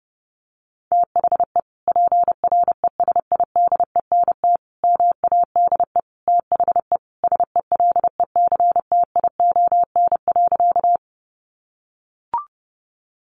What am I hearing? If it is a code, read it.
Morse code, heard as a keyed tone, about 30 wpm, 700 Hz. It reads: THE PRESIDENT MADE THE SELECTION.
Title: Sentences from Top 1400 Words 30wpm